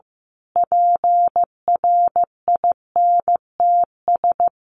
Morse code, heard as a keyed tone, about 15 wpm, 700 Hz. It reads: PRINTS
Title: A Sound Of Thunder 15wpm